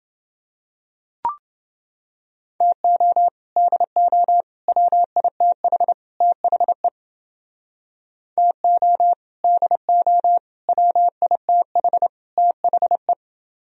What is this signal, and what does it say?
Morse code, heard as a keyed tone, about 30 wpm, 700 Hz. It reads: TO DO WITH THE TO DO WITH THE